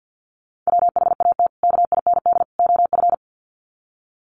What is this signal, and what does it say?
Morse code, heard as a keyed tone, about 48 wpm, 700 Hz. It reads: WHAT KIND OF
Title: Sets of 3 Words 50wpm